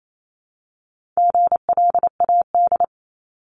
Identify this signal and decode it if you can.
Morse code, heard as a keyed tone, about 28 wpm, 700 Hz. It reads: GLAD
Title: Multi-letter Phonemes 28wpm